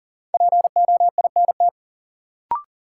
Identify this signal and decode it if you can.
Morse code, heard as a keyed tone, about 40 wpm, 700 Hz. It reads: POINT
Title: Sentences from Top 200 Words - New Word Drill - 40wpm